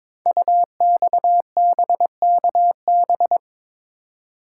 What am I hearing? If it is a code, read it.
Morse code, heard as a keyed tone, about 22 wpm, 700 Hz. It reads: UXBKB
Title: Code Groups - 5 Letters - 22wpm